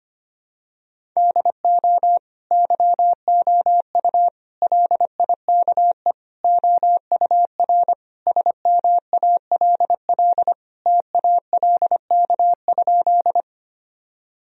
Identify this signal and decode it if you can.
Morse code, heard as a keyed tone, about 25 wpm, 700 Hz. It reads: DO YOU LIKE OUR SMALL TALK?
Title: Sentences from Top 1000 Words - Review Part 1 - 25wpm